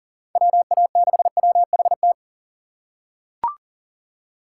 Morse code, heard as a keyed tone, about 40 wpm, 700 Hz. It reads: WA6WHT
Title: USA Call Signs - 6 Characters - 3x Character Spacing - 40wpm